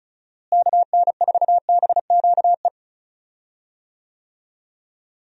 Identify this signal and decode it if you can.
Morse code, heard as a keyed tone, about 34 wpm, 700 Hz. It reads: KN4BQE
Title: USA Call Signs - 6 Characters - Repeated 2x - 35wpm